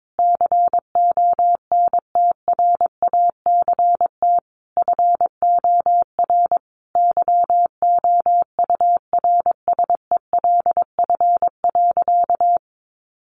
Morse code, heard as a keyed tone, about 22 wpm, 700 Hz. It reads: CONTRACT FOR YOURSELF.
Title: Sentences from Top 1400 Words 22wpm